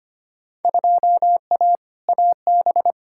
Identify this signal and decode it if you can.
Morse code, heard as a keyed tone, about 25 wpm, 700 Hz. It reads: 2A AB
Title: Field Day Exchanges - 25wpm